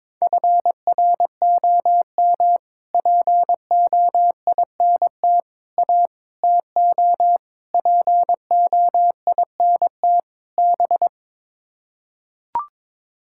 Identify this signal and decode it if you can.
Morse code, heard as a keyed tone, about 22 wpm, 700 Hz. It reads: FROM POINT A TO POINT B
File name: practice / 22wpm / Sets of 6 Words - 22wpm